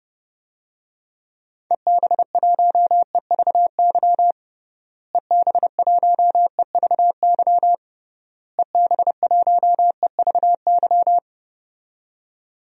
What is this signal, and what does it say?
Morse code, heard as a keyed tone, about 30 wpm, 700 Hz. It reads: EB1EVY EB1EVY EB1EVY
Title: DXCC-100 Call Signs - 6 Characters - Repeated 3x - 30wpm